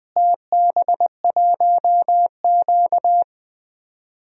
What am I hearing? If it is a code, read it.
Morse code, heard as a keyed tone, about 20 wpm, 700 Hz. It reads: TB1Q
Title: DXCC-100 Call Signs - 4 Characters - Repeated 2x - 20wpm